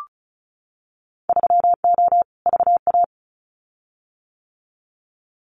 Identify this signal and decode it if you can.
Morse code, heard as a keyed tone, about 36 wpm, 700 Hz. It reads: 3O VA
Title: Winter Field Day Exchanges 35wpm